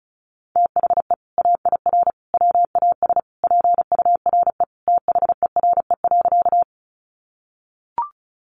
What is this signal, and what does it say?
Morse code, heard as a keyed tone, about 34 wpm, 700 Hz. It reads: THE AIR WAS PURE THERE.
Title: Sentences from Top 1500 Words 35wpm